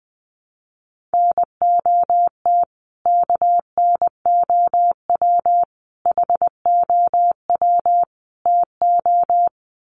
Morse code, heard as a keyed tone, about 20 wpm, 700 Hz. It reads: NOT KNOW HOW TO
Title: Sets of 4 Words 20wpm